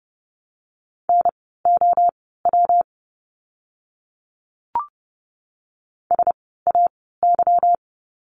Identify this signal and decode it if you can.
Morse code, heard as a keyed tone, about 30 wpm, spaced as for 17 wpm, 700 Hz. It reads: NOW SAY
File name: course / 30:10wpm / 230 Lesson 37f - Letter x - Words 30wpm